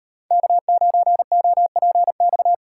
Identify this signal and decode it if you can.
Morse code, heard as a keyed tone, about 33 wpm, 700 Hz. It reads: K9OPX
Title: Speed Racing - USA 1x3 Call Signs 28wpm